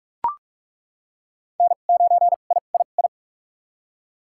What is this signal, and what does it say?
Morse code, heard as a keyed tone, about 45 wpm, spaced as for 32 wpm, 700 Hz. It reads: N9III
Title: USA Call Signs - 5 Characters - 2x Character Spacing - 45wpm